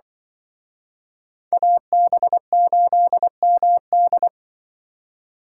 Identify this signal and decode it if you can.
Morse code, heard as a keyed tone, about 24 wpm, 700 Hz. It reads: AB8MD
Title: Speed Racing - USA 2x2 Call Signs 20wpm